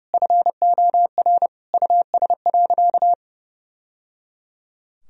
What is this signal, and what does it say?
Morse code, heard as a keyed tone, about 30 wpm, 700 Hz. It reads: FOR US.
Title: Sentences from Top 1000 Words - Review Part 3 - 30wpm